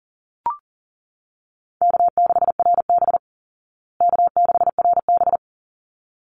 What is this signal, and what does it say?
Morse code, heard as a keyed tone, about 40 wpm, 700 Hz. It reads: K6RB K6RB
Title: Easy Call Signs Repeated 2x - 40wpm